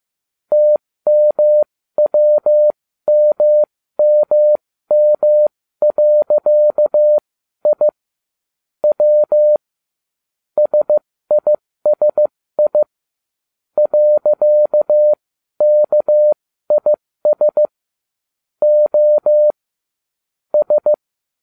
Morse code, heard as a keyed tone, about 15 wpm, spaced as for 13 wpm, 600 Hz. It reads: TMWMMM.I W SISI .KIS O S